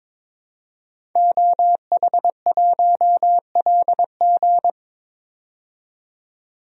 Morse code, heard as a keyed tone, about 22 wpm, 700 Hz. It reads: OH1LG
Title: DXCC-100 Call Signs - 5 Characters - 22wpm